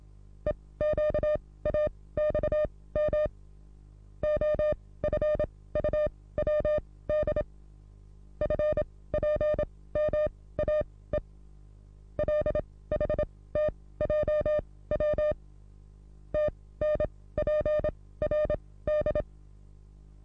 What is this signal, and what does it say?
Morse code, heard as a keyed tone, about 27 wpm, spaced as for 18 wpm, 600 Hz. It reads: EQAXM OFUWD FPMAE LHTJW TNPRD
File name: repos / Morsekurs / DJ4UF Morsen27